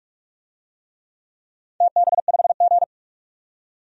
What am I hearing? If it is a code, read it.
Morse code, heard as a keyed tone, about 42 wpm, 700 Hz. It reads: TB5G
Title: DXCC-100 Call Signs - 4 Characters - 45wpm